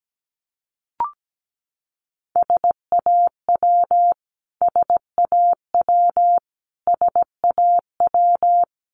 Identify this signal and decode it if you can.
Morse code, heard as a keyed tone, about 17 wpm, 700 Hz. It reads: SAW SAW SAW
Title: Top 300 Words Repeated 17wpm